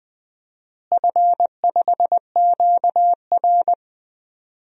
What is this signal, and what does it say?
Morse code, heard as a keyed tone, about 21 wpm, 700 Hz. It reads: F5QR
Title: DXCC-100 Call Signs - 4 Characters - Repeated 2x - 20wpm